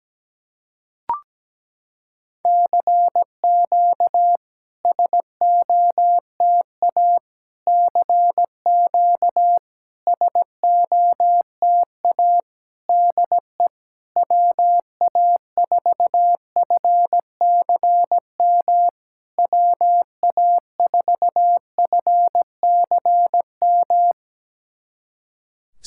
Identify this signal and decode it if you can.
Morse code, heard as a keyed tone, about 17 wpm, 700 Hz. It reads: CQ SOTA CQ SOTA DE WA4FCM WA4FCM
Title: SOTA QSO Practice 17wpm